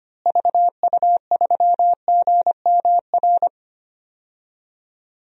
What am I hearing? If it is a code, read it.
Morse code, heard as a keyed tone, about 25 wpm, 700 Hz. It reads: VU3GMR
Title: DXCC-100 Call Signs - 6 Characters - Repeated 2x - 25wpm